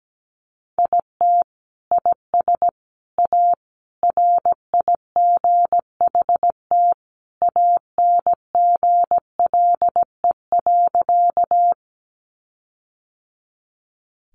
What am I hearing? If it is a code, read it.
Morse code, heard as a keyed tone, about 17 wpm, 700 Hz. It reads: IT IS A RIGHT ANGLE.
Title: Sentences from Top 1500 Words 17wpm